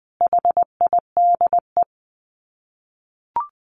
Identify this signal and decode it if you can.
Morse code, heard as a keyed tone, about 20 wpm, 700 Hz. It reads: HIDE
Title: Dit Challenge 20wpm